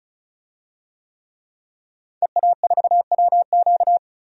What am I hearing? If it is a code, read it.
Morse code, heard as a keyed tone, about 36 wpm, 700 Hz. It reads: EA4WQ
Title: DXCC-100 Call Signs - 5 Characters - 35wpm